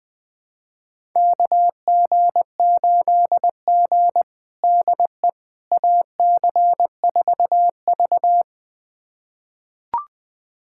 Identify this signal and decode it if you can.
Morse code, heard as a keyed tone, about 19 wpm, 700 Hz. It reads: KG8G DE AC4V
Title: USA Callsign DE Callsign - 4 Characters - 20wpm